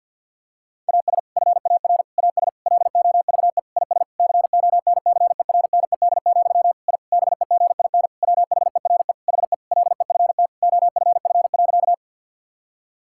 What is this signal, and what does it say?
Morse code, heard as a keyed tone, about 50 wpm, 700 Hz. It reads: AS FAR AS LOVE IS CONCERNED, I BEGIN WHERE HE LEFT OFF.
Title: Sentences from Top 1000 Words - Review Part 6 - 50wpm